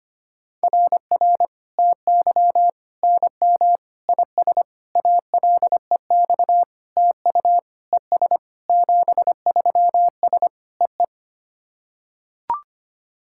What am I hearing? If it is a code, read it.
Morse code, heard as a keyed tone, about 25 wpm, 700 Hz. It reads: RR TY NM IS ALEX TU ES 73S EE
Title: SOTA QSO Practice 25wpm